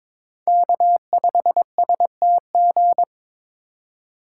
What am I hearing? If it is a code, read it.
Morse code, heard as a keyed tone, about 26 wpm, 700 Hz. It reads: K5STG
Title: Speed Racing - USA 1x3 Call Signs 22wpm